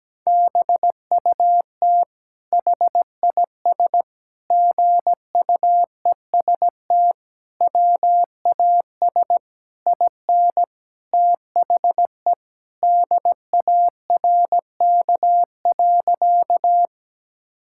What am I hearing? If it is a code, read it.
Morse code, heard as a keyed tone, about 17 wpm, 700 Hz. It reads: BUT HIS GUEST WAS IN THE DARK.
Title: Sentences from Top 1600 Words 17wpm